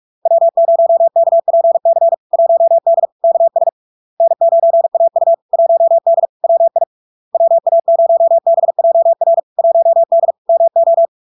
Morse code, heard as a keyed tone, about 46 wpm, 670 Hz. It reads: W0KPC 1D KS N9AU 1D WI WA0BJR 1D MO
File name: LARK-FD-2022-Log-in-CW-45wpm-670-Hz